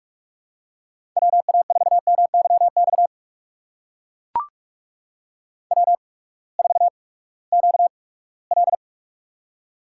Morse code, heard as a keyed tone, about 43 wpm, spaced as for 22 wpm, 700 Hz. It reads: WA4MYX W4QL
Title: USA Call Signs - 8x Character Spacing - 45wpm